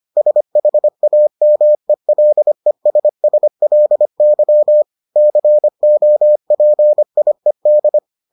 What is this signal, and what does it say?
Morse code, heard as a keyed tone, about 25 wpm, 600 Hz. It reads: SHAMELESSLY COPIED